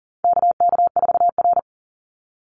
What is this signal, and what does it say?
Morse code, heard as a keyed tone, about 40 wpm, 700 Hz. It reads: KK4R
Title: Call Signs Repeated 2x - 40wpm